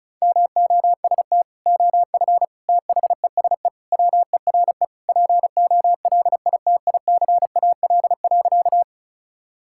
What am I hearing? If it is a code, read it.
Morse code, heard as a keyed tone, about 36 wpm, 700 Hz. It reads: MOST OF THESE WERE POLITICAL.
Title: Sentences from Top 1000 Words - Review Part 5 - 35wpm